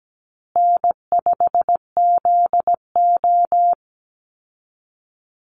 Morse code, heard as a keyed tone, about 17 wpm, 700 Hz. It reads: N5ZO
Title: Call Signs Repeated 3x - 17wpm